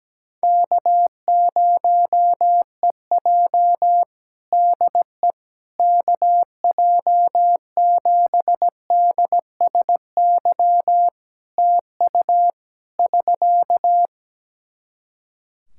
Morse code, heard as a keyed tone, about 17 wpm, 700 Hz. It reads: K0EJ DE KJ7DSY TU <SK>
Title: Practice QSO - Exchange 4 - Ending - 17wpm